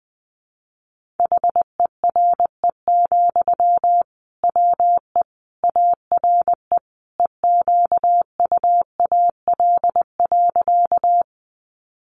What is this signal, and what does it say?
Morse code, heard as a keyed tone, about 20 wpm, 700 Hz. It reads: HERE, WE ARE EQUAL.
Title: Sentences from Top 1400 Words 20wpm